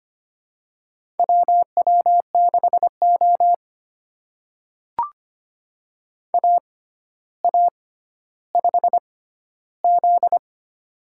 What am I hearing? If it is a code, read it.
Morse code, heard as a keyed tone, about 25 wpm, spaced as for 13 wpm, 700 Hz. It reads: WW6O AA5Z
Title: USA Call Signs - 4 Characters - 6x Character Spacing - 25wpm